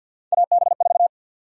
Morse code, heard as a keyed tone, about 52 wpm, 700 Hz. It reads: AB4
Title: USA Call Sign Prefixes - 50wpm